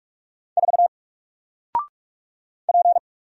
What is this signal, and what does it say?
Morse code, heard as a keyed tone, about 46 wpm, 700 Hz. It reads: 4 P
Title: Single Letter-Number Alt 45wpm